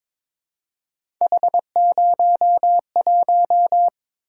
Morse code, heard as a keyed tone, about 22 wpm, 700 Hz. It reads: H01
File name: practice / 22wpm / Three Letter-Number Combinations - 22wpm